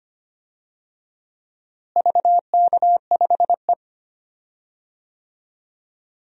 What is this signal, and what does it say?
Morse code, heard as a keyed tone, about 25 wpm, 700 Hz. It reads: VK5E
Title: DXCC-100 Call Signs - 4 Characters - Repeated 2x - 25wpm